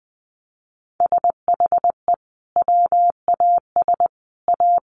Morse code, heard as a keyed tone, about 20 wpm, 700 Hz. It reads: SHE WAS A
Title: Sentences from Top 1300 Words 20wpm